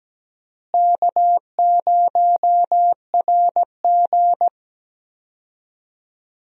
Morse code, heard as a keyed tone, about 17 wpm, 700 Hz. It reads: K0RG
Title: USA Call Signs - 4 Characters - Repeated 2x - 17wpm